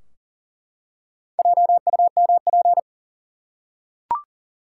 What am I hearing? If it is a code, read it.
Morse code, heard as a keyed tone, about 40 wpm, 700 Hz. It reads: JUMP
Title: Four Letter Words - 40wpm